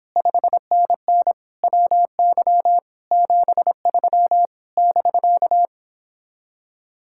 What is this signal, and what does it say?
Morse code, heard as a keyed tone, about 29 wpm, 700 Hz. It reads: 5NN WY 73 <BK>